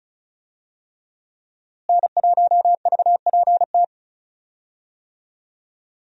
Morse code, heard as a keyed tone, about 36 wpm, 700 Hz. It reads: N1VPT
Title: USA Call Signs - 5 Characters - Repeated 3x - 35wpm